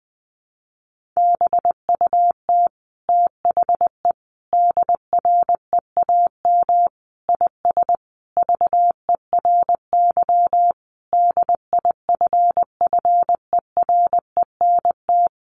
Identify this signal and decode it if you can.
Morse code, heard as a keyed tone, about 20 wpm, 700 Hz. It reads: BUT THE DREAM IS VERY DIFFERENT
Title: Sentences from Top 1500 Words 20wpm